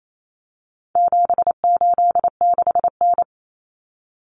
Morse code, heard as a keyed tone, about 28 wpm, 700 Hz. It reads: 786N
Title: Common Cut Numbers - 4-Digit 28wpm